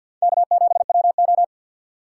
Morse code, heard as a keyed tone, about 51 wpm, 700 Hz. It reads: K7WO
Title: USA Call Signs - 3x Character Spacing - 50wpm